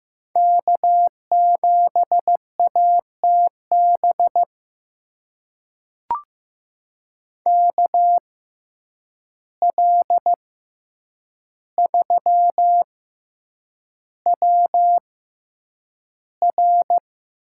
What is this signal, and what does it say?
Morse code, heard as a keyed tone, about 15 wpm, spaced as for 8 wpm, 700 Hz. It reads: K7ATB KL3WR
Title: USA Call Signs - 5 Characters - 6x Character Spacing - 15wpm